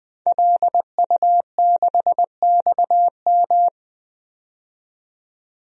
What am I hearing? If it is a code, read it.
Morse code, heard as a keyed tone, about 20 wpm, 700 Hz. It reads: LU6XM
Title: DXCC-100 Call Signs - 5 Characters - 20wpm